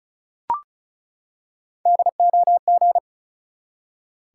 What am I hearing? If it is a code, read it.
Morse code, heard as a keyed tone, about 36 wpm, 700 Hz. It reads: DOG
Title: Three Letter Words - 35wpm